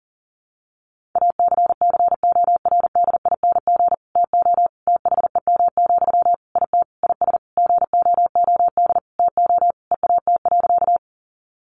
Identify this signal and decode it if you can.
Morse code, heard as a keyed tone, about 40 wpm, 700 Hz. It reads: ACCORDING TO THEM, IT IS GOOD TO EAT.
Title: Sentences from Top 800 Words 40wpm